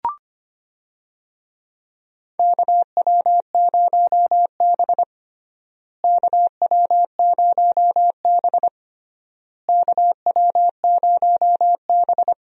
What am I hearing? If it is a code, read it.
Morse code, heard as a keyed tone, about 25 wpm, 700 Hz. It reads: KW0B KW0B KW0B
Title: USA Call Signs - 4 Characters - Repeated 3x - 25wpm